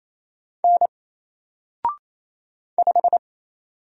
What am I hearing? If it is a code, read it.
Morse code, heard as a keyed tone, about 27 wpm, 700 Hz. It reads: N 5
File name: Single-Letter-Number-28wpm